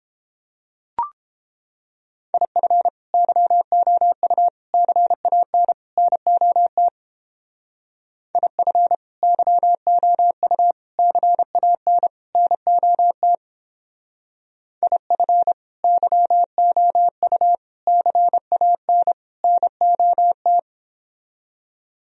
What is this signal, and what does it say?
Morse code, heard as a keyed tone, about 29 wpm, 700 Hz. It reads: IF YOU CAN NOT IF YOU CAN NOT IF YOU CAN NOT